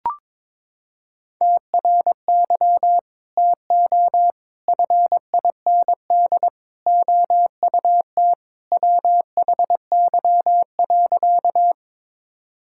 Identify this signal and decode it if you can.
Morse code, heard as a keyed tone, about 21 wpm, 700 Hz. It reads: TRY TO FIND OUT WHY.
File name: practice / 22wpm / Sentences with 3-Word Phrases - 22wpm